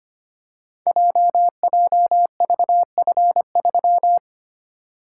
Morse code, heard as a keyed tone, about 25 wpm, 700 Hz. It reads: JJVF3
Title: Code Groups - 5 Letter:Numbers - 25wpm